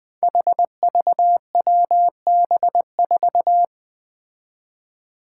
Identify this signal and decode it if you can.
Morse code, heard as a keyed tone, about 21 wpm, 700 Hz. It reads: HVWB4
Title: Five Challenging Letter-Number Combinations 20wpm